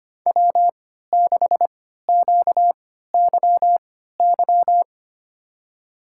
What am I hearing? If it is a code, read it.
Morse code, heard as a keyed tone, about 25 wpm, spaced as for 14 wpm, 700 Hz. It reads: W6QYY
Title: USA Call Signs - 3x Character Spacing - 25wpm